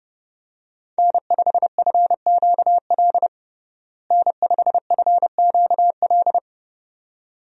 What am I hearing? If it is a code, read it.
Morse code, heard as a keyed tone, about 30 wpm, 700 Hz. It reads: N5FQL N5FQL
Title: USA Call Signs - 5 Characters - Repeated 2x - 30wpm